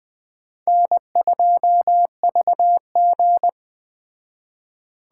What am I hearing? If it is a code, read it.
Morse code, heard as a keyed tone, about 18 wpm, 700 Hz. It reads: N2VG
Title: Speed Racing - USA 1x2 Call Signs 15wpm